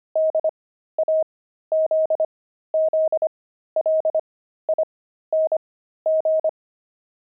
Morse code, heard as a keyed tone, about 25 wpm, spaced as for 13 wpm, 630 Hz. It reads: DAZZLING
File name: pangram-fs-13-cs-25-jelly